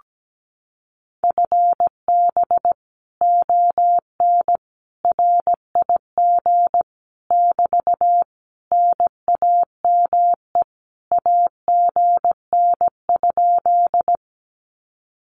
Morse code, heard as a keyed tone, about 17 wpm, 700 Hz. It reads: FB ON RIG <BT> NAME AGN?
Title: Common QSO Phrases Encore 17wpm